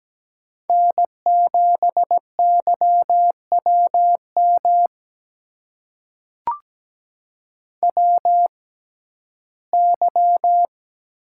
Morse code, heard as a keyed tone, about 17 wpm, spaced as for 12 wpm, 700 Hz. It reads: N7YWM WY
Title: USA Call Signs - 6x Character Spacing - 17wpm